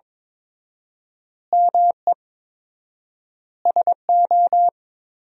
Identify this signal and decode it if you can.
Morse code, heard as a keyed tone, about 21 wpm, 690 Hz. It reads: ME SO
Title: 4x Word-Spacing - Sentences from Top 400 Words - 22wpm